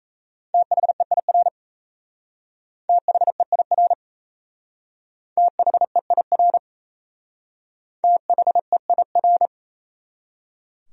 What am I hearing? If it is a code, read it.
Morse code, heard as a keyed tone, about 36 wpm, 700 Hz. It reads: THEIR THEIR THEIR THEIR